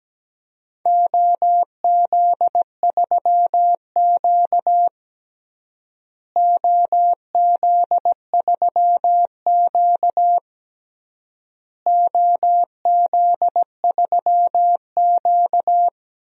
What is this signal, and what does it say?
Morse code, heard as a keyed tone, about 17 wpm, 700 Hz. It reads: OZ3Q OZ3Q OZ3Q
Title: DXCC-100 Call Signs - 4 Characters - Repeated 3x - 17wpm